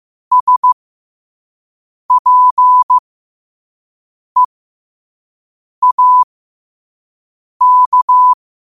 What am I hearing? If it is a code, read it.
Morse code, heard as a keyed tone, about 15 wpm, spaced as for 5 wpm, 1000 Hz. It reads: SPEAK
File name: qotdfiles / 5wpm